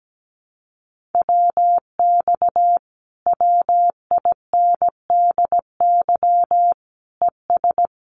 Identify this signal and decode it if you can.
Morse code, heard as a keyed tone, about 17 wpm, 700 Hz. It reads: WX WINDY ES
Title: Common QSO Phrases Encore 17wpm